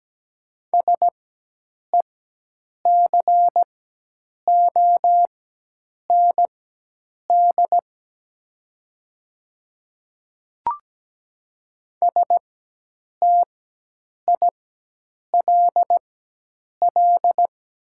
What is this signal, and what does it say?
Morse code, heard as a keyed tone, about 17 wpm, spaced as for 8 wpm, 700 Hz. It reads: SECOND STILL
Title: 4x Character Spacing - Top 300 Words - 17wpm